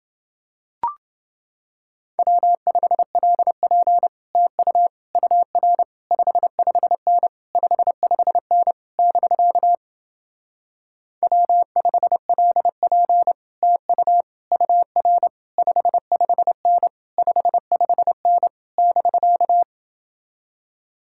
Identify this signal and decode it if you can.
Morse code, heard as a keyed tone, about 30 wpm, 700 Hz. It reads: W5LP TU UR 55N 55N <BK> W5LP TU UR 55N 55N <BK>